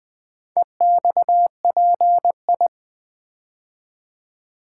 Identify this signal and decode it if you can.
Morse code, heard as a keyed tone, about 20 wpm, 700 Hz. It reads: EXPI
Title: Build A Word - 2 Letters At A Time - 6-Letter Words - 20wpm